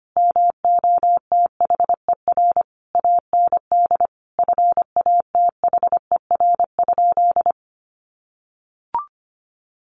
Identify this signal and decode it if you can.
Morse code, heard as a keyed tone, about 25 wpm, 700 Hz. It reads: MOTHER AND FATHER?
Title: Sentences from Top 1200 Words 25wpm